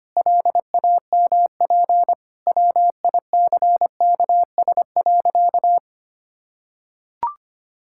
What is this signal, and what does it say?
Morse code, heard as a keyed tone, about 25 wpm, 700 Hz. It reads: LAMP WICKS.
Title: Pangram Challenge v2 - 25wpm